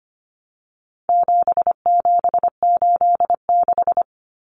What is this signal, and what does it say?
Morse code, heard as a keyed tone, about 25 wpm, 700 Hz. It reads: 7786
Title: Numbers - 4-Digit 25wpm